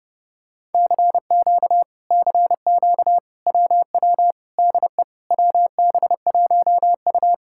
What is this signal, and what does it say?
Morse code, heard as a keyed tone, about 30 wpm, 700 Hz. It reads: CQ CQ WW DE WB1U